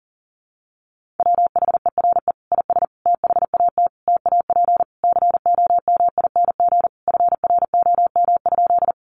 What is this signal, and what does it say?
Morse code, heard as a keyed tone, about 37 wpm, 700 Hz. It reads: WHERE IS THAT TAP COMING FROM?
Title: Sentences from Top 1400 Words 40wpm